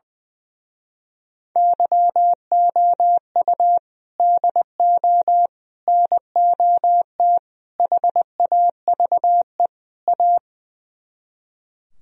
Morse code, heard as a keyed tone, about 20 wpm, 700 Hz. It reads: YOU DO NOT HAVE A
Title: Speed-Racing - Sets of 5 Words 20wpm